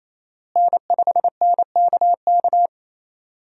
Morse code, heard as a keyed tone, about 28 wpm, 700 Hz. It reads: N5NKK
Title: USA Call Signs - 5 Characters - Repeated 2x - 28wpm